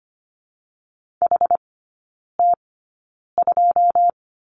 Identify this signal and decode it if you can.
Morse code, heard as a keyed tone, about 25 wpm, 700 Hz. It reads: H T 2
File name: Lesson 17e - Letter u - Three Characters 25wpm